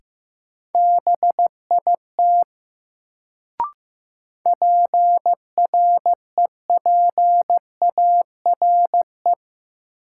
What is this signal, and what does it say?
Morse code, heard as a keyed tone, about 15 wpm, 700 Hz. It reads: BIT PREPARE
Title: Sentences from Top 900 Words - New Word Drill - 15wpm